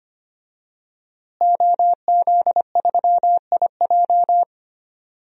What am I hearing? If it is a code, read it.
Morse code, heard as a keyed tone, about 25 wpm, 700 Hz. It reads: OZ3IJ
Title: DXCC-100 Call Signs - 5 Characters - Repeated 3x - 25wpm